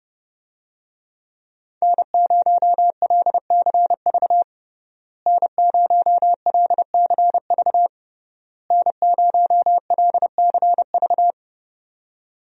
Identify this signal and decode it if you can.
Morse code, heard as a keyed tone, about 30 wpm, 700 Hz. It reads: N0LCV N0LCV N0LCV
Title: USA Call Signs - 5 Characters - Repeated 3x - 30wpm